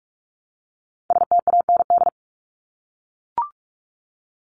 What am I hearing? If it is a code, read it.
Morse code, heard as a keyed tone, about 44 wpm, 700 Hz. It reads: STAND
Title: Top 500 Words 45wpm